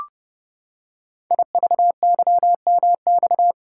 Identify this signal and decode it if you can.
Morse code, heard as a keyed tone, about 28 wpm, 700 Hz. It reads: IVYMX
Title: Code Groups - 5 Letters - 30wpm